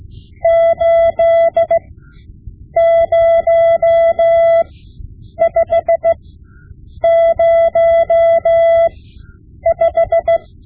Morse code, heard as a keyed tone, about 16 wpm, spaced as for 8 wpm, 670 Hz. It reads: 80505
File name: Morse Lessons / Morse Lessons / Tape-5B